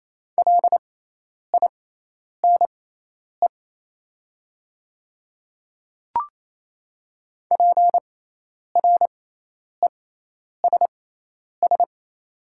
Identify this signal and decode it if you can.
Morse code, heard as a keyed tone, about 28 wpm, spaced as for 10 wpm, 700 Hz. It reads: LINE PRESS
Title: 6x Character Spacing - Top 300 Words - 28wpm